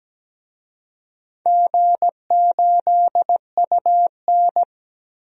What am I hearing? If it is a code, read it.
Morse code, heard as a keyed tone, about 17 wpm, 700 Hz. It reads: G8UN
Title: DXCC-100 Call Signs - 4 Characters - 17wpm